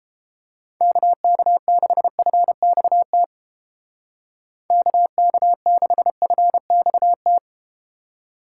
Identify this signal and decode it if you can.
Morse code, heard as a keyed tone, about 31 wpm, 700 Hz. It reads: KK6FXT KK6FXT